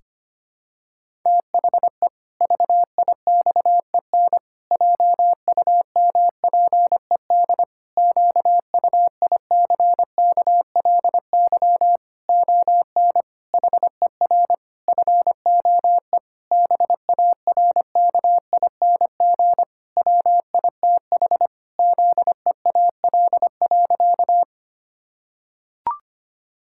Morse code, heard as a keyed tone, about 25 wpm, 700 Hz. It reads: THE VIXEN JUMPED QUICKLY ON HER FOE BARKING WITH ZEAL.
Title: Pangram Challenge v2 - 25wpm